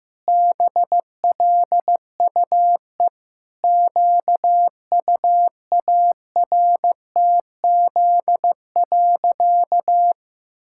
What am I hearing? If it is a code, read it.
Morse code, heard as a keyed tone, about 16 wpm, 700 Hz. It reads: BLUE QUARTZ.
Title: Pangram Challenge v2 - 15wpm